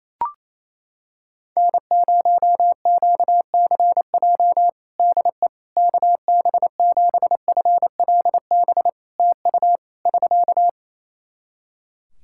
Morse code, heard as a keyed tone, about 27 wpm, 700 Hz. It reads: N0QCJ DE KB7FLB TU <SK>
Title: Practice QSO - Exchange 4 - Ending - 28wpm